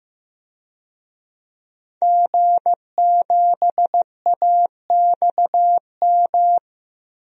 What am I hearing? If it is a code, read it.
Morse code, heard as a keyed tone, about 16 wpm, 700 Hz. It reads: G7AXM
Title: DXCC-100 Call Signs - 5 Characters - Repeated 2x - 15wpm